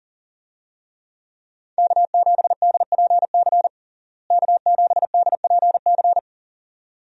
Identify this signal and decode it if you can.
Morse code, heard as a keyed tone, about 40 wpm, 700 Hz. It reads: K7DPC K7DPC
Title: USA Call Signs - 5 Characters - Repeated 2x - 40wpm